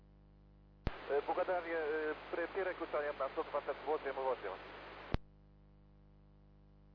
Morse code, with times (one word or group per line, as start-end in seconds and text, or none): none